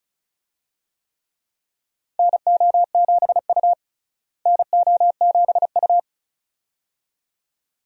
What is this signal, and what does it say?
Morse code, heard as a keyed tone, about 36 wpm, 700 Hz. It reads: NO7U NO7U
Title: USA Call Signs - 4 Characters - Repeated 2x - 35wpm